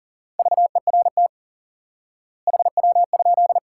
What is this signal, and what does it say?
Morse code, heard as a keyed tone, about 40 wpm, 700 Hz. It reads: VERT HW?
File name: IQE Course 3 - Lesson 89 - YAGI - Two Elements - 40wpm